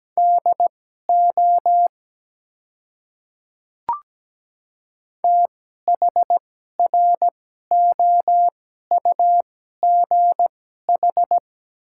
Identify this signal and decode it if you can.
Morse code, heard as a keyed tone, about 17 wpm, spaced as for 12 wpm, 700 Hz. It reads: DO THROUGH
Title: 2x Character Spacing - Top 200 Words - 17wpm